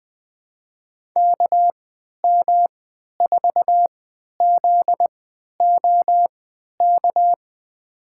Morse code, heard as a keyed tone, about 20 wpm, spaced as for 11 wpm, 700 Hz. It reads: KM4ZOK
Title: USA Call Signs - 6 Characters - 3x Character Spacing - 20wpm